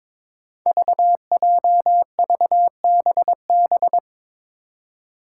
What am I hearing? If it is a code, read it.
Morse code, heard as a keyed tone, about 22 wpm, 700 Hz. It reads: VJVBB
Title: Five Challenging Letter-Number Combinations 22wpm